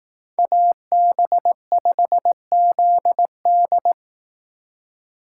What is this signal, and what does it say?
Morse code, heard as a keyed tone, about 18 wpm, 700 Hz. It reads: AB5ZD
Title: Speed Racing - USA 2x2 Call Signs 15wpm